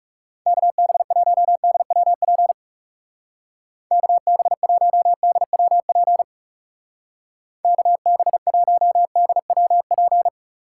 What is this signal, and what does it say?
Morse code, heard as a keyed tone, about 36 wpm, 700 Hz. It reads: KB1DWP KB1DWP KB1DWP
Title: Speed Racing - USA Call Signs 30wpm